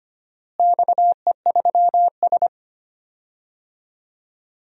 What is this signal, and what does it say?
Morse code, heard as a keyed tone, about 25 wpm, 700 Hz. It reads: XE3S
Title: DXCC-100 Call Signs - 4 Characters - Repeated 3x - 25wpm